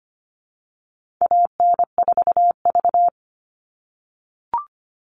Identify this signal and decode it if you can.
Morse code, heard as a keyed tone, about 25 wpm, 700 Hz. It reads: AN4V
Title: All Cut Numbers - 4-Digit 25wpm